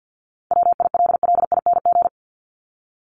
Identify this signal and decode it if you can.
Morse code, heard as a keyed tone, about 50 wpm, 700 Hz. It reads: WILLING
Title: Top 751-1000 Words 50wpm